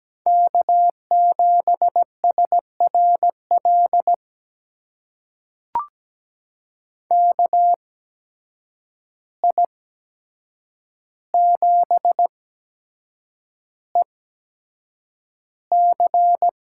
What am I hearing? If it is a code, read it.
Morse code, heard as a keyed tone, about 17 wpm, spaced as for 7 wpm, 700 Hz. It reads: K7SRL KI7EC
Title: USA Call Signs - 5 Characters - 8x Character Spacing - 17wpm